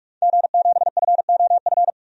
Morse code, heard as a keyed tone, about 42 wpm, 700 Hz. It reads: G7FOF
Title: DXCC-100 Call Signs - 5 Characters - 45wpm